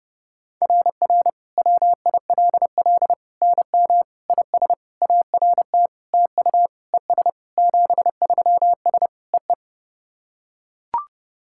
Morse code, heard as a keyed tone, about 30 wpm, 690 Hz. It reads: RR WILL NM IS ART TU ES 73S EE
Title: SOTA QSO Practice 30wpm